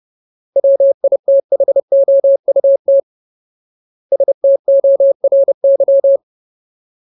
Morse code, heard as a keyed tone, about 30 wpm, 550 Hz. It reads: WITHOUT STORY
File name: Ultimate Word List / 30WPM Ultimate